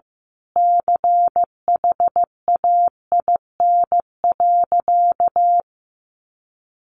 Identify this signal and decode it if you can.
Morse code, heard as a keyed tone, about 15 wpm, 700 Hz. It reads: CHAIN.
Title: Sentences from Top 1500 Words 15wpm